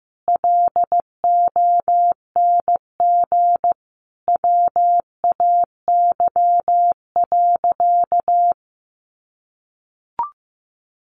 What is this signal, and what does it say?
Morse code, heard as a keyed tone, about 15 wpm, 700 Hz. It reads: LONG WAY.
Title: Sentences from Top 600 Words 15wpm